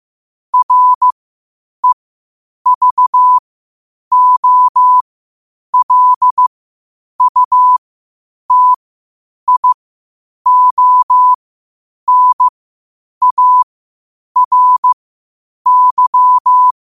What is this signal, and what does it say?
Morse code, heard as a keyed tone, about 15 wpm, spaced as for 8 wpm, 1000 Hz. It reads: REVOLUTIONARY
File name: qotdfiles / 8wpm